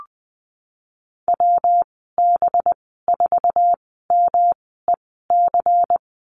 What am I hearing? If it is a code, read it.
Morse code, heard as a keyed tone, about 20 wpm, spaced as for 15 wpm, 700 Hz. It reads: WB4MEC
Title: 186 Lesson 30g - Letter b - Call Signs 20wpm